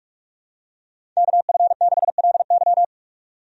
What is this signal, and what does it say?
Morse code, heard as a keyed tone, about 46 wpm, 700 Hz. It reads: KF6LY
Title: USA Call Signs - 5 Characters - 45wpm